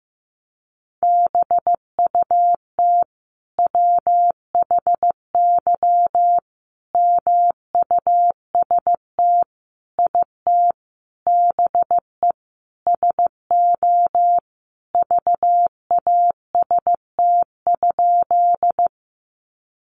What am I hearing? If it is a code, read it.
Morse code, heard as a keyed tone, about 15 wpm, 700 Hz. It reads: BUT WHY MUST IT BE SO VAST?
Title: Sentences from Top 1300 Words 15wpm